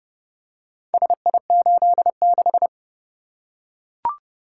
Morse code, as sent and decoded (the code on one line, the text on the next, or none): ... .. ---.. -....
SI86